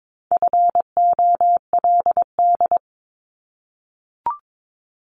Morse code, heard as a keyed tone, about 22 wpm, 700 Hz. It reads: FOLD
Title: Multi-letter Phonemes 22wpm